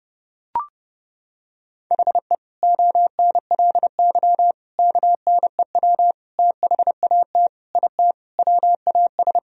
Morse code, heard as a keyed tone, about 30 wpm, 700 Hz. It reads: HE ONLY KNEW THAT IT WAS
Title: Sentences from Top 1000 Words - Review Part 2 - 30wpm